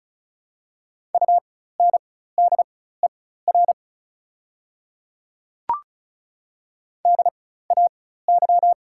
Morse code, heard as a keyed tone, about 36 wpm, spaced as for 16 wpm, 690 Hz. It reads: UNDER DAY
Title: 4x Character Spacing - Top 500 Words - 35wpm